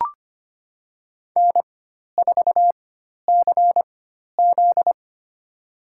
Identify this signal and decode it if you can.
Morse code, heard as a keyed tone, about 25 wpm, spaced as for 12 wpm, 700 Hz. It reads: N4CZ
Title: USA Call Signs - 4 Characters - 4x Character Spacing - 25wpm